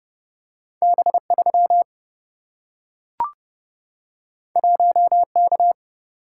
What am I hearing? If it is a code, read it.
Morse code, heard as a keyed tone, about 30 wpm, 700 Hz. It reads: B 3 1 K
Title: Single-Letter Single-Number 30wpm